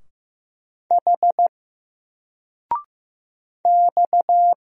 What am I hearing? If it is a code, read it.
Morse code, heard as a keyed tone, about 15 wpm, 700 Hz. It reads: H X
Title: Single Letter-Number 15wpm